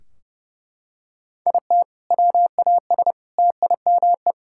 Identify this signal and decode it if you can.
Morse code, heard as a keyed tone, about 32 wpm, 700 Hz. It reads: IT WAS TIME
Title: Sentences from Top 1600 Words 30wpm